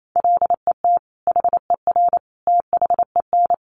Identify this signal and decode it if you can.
Morse code, heard as a keyed tone, about 28 wpm, 700 Hz. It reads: LET HER THEN
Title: Sentences from Top 1500 Words 28wpm